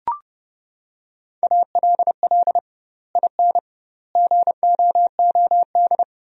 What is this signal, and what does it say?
Morse code, heard as a keyed tone, about 32 wpm, 700 Hz. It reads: ALL IN GOOD
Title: 2x Word-Spacing - Sentences from Top 200 Words - 30wpm